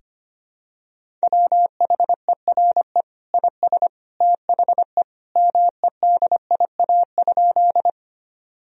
Morse code, heard as a keyed tone, about 25 wpm, 700 Hz. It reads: WHERE IS THE MEDIA?
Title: Sentences from Top 1000 Words - Review Part 4 - 25wpm